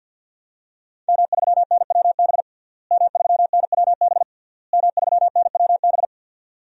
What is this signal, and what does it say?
Morse code, heard as a keyed tone, about 51 wpm, 700 Hz. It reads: M3NWB M3NWB M3NWB
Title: DXCC-100 Call Signs - 5 Characters - Repeated 3x - 50wpm